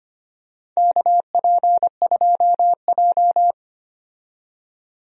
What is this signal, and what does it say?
Morse code, heard as a keyed tone, about 25 wpm, 700 Hz. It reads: KP2J
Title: DXCC-100 Call Signs - 4 Characters - 25wpm